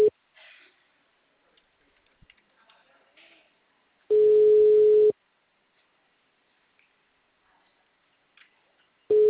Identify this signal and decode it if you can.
Ringback tone (Europe, 425 Hz).